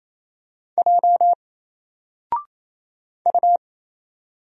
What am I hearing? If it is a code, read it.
Morse code, heard as a keyed tone, about 27 wpm, 700 Hz. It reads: J U
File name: Single-Letter-Number-28wpm